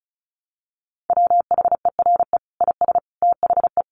Morse code, heard as a keyed tone, about 34 wpm, 700 Hz. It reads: WHERE IS THE
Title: Sentences from Top 1200 Words 35wpm